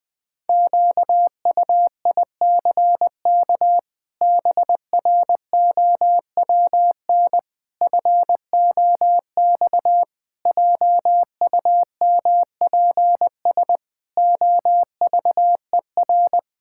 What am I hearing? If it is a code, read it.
Morse code, heard as a keyed tone, about 21 wpm, 700 Hz. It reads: QUICK BROWN FOX JUMPS OVER
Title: Sets of 8 Words - 20wpm